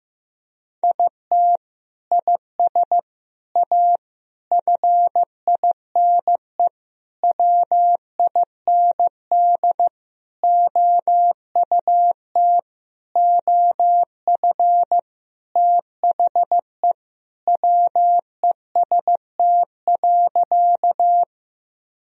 Morse code, heard as a keyed tone, about 15 wpm, 700 Hz. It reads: IT IS A FINE WIND OUT OF THE WEST.
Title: Sentences from Top 1000 Words - Review Part 2 - 15wpm